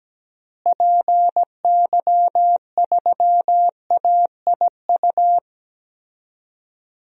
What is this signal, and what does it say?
Morse code, heard as a keyed tone, about 17 wpm, 700 Hz. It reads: PY3AIU
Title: DXCC-100 Call Signs - 6 Characters - Repeated 3x - 17wpm